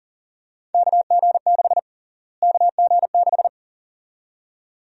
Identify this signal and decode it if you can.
Morse code, heard as a keyed tone, about 40 wpm, 700 Hz. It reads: KG6 KG6
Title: USA Call Sign Prefixes - Repeated 2x - 40wpm